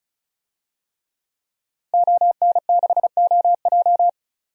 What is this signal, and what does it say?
Morse code, heard as a keyed tone, about 36 wpm, 700 Hz. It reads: ON6OJ
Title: DXCC-100 Call Signs - 5 Characters - Repeated 3x - 35wpm